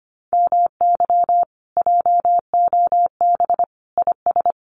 Morse code, heard as a keyed tone, about 25 wpm, 700 Hz. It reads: MY JOB IS
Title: Common QSO Phrases Encore 25wpm